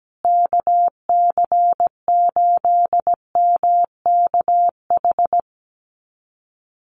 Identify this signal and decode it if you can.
Morse code, heard as a keyed tone, about 17 wpm, 700 Hz. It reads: KC8MKH
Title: USA Call Signs Repeated 2x 17wpm